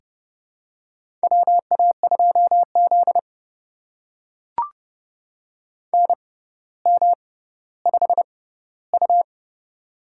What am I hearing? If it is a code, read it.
Morse code, heard as a keyed tone, about 30 wpm, spaced as for 15 wpm, 700 Hz. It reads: WA2Z NM5U
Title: USA Call Signs - 4 Characters - 6x Character Spacing - 30wpm